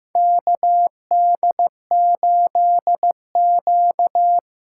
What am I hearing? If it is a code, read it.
Morse code, heard as a keyed tone, about 15 wpm, 700 Hz. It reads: KD8Q
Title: USA Call Signs - 6 Characters - 8x Character Spacing - 15wpm